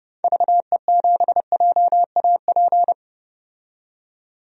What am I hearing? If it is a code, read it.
Morse code, heard as a keyed tone, about 30 wpm, 700 Hz. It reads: VE7JAP
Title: DXCC-100 Call Signs - 6 Characters - 30wpm